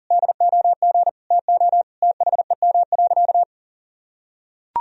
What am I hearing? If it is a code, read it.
Morse code, heard as a keyed tone, about 40 wpm, 700 Hz. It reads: DOG TO THEM.
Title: Sentences from Top 1000 Words - Review Part 2 - 40wpm